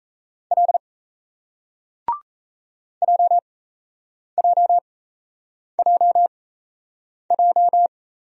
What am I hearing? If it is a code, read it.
Morse code, heard as a keyed tone, about 37 wpm, spaced as for 8 wpm, 700 Hz. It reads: L JJJJ